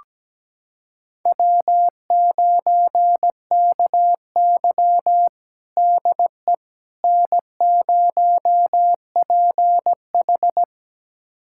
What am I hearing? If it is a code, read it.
Morse code, heard as a keyed tone, about 17 wpm, 700 Hz. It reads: W9KY DE N0PH
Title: USA Callsign DE Callsign - 4 Characters - 17wpm